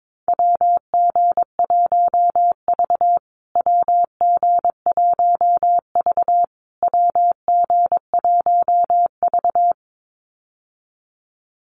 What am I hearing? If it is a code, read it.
Morse code, heard as a keyed tone, about 22 wpm, 700 Hz. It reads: WG1V WG1V WG1V
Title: USA Call Signs Repeated 3x 22wpm